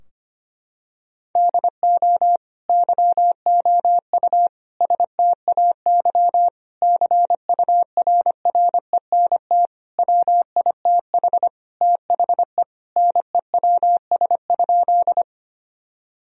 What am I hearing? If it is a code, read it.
Morse code, heard as a keyed tone, about 25 wpm, 690 Hz. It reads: DO YOU STAY CURRENT WITH THE NEWS?
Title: Sentences from Top 1000 Words - Review Part 4 - 25wpm